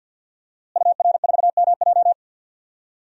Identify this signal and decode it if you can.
Morse code, heard as a keyed tone, about 52 wpm, 700 Hz. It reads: UR4MJ
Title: DXCC-100 Call Signs - 5 Characters - Repeated 2x - 50wpm